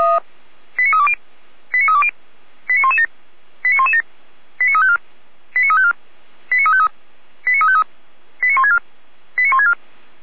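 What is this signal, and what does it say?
Five-tone selective call (ZVEI), heard as quick runs of five tones, tones of about 70 ms. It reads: 89320 89320 89117 89117 89354 89354 89353 89353 89164 89164
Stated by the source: ZVEI five-tone call 89126 received on an SDR.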